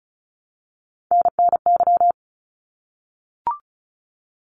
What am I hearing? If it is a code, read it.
Morse code, heard as a keyed tone, about 34 wpm, 700 Hz. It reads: NNY
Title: ARRL Sections 35wpm